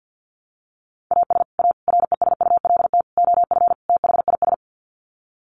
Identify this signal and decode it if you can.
Morse code, heard as a keyed tone, about 50 wpm, 700 Hz. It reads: AS A RESULT OF THIS
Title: Sets of 5 Words 50wpm